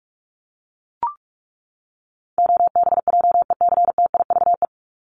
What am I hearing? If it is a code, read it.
Morse code, heard as a keyed tone, about 44 wpm, 700 Hz. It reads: OBJECTIVE
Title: Top 1001-1250 Words 45wpm